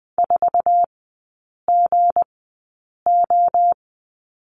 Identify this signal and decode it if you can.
Morse code, heard as a keyed tone, about 20 wpm, spaced as for 8 wpm, 700 Hz. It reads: 4GO
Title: 166 Lesson 27e - Number 7 - Three Characters 20wpm